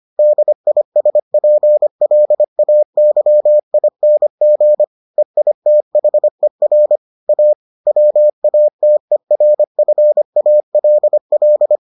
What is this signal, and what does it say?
Morse code, heard as a keyed tone, about 25 wpm, 600 Hz. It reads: DISPLAYING EITHER A WATERFALL